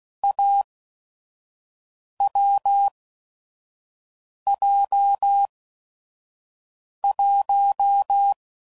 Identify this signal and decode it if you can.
Morse code, heard as a keyed tone, about 16 wpm, spaced as for 5 wpm, 800 Hz. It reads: AWJ1
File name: Combinations / Difficult Combos / awj1 ndb6 gz7